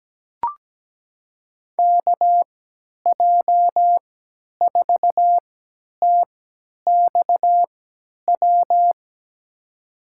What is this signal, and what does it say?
Morse code, heard as a keyed tone, about 17 wpm, spaced as for 10 wpm, 700 Hz. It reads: KJ4TXW
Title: USA Call Signs - 6 Characters - 3x Character Spacing - 17wpm